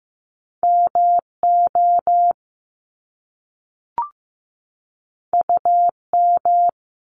Morse code, heard as a keyed tone, about 15 wpm, 700 Hz. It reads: MO UM
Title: Top Two Letter Combinations in Words 15wpm